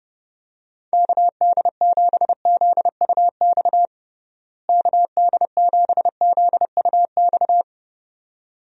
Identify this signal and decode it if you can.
Morse code, heard as a keyed tone, about 30 wpm, 700 Hz. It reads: KD7ZUX KD7ZUX
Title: USA Call Signs - 6 Characters - Repeated 2x - 30wpm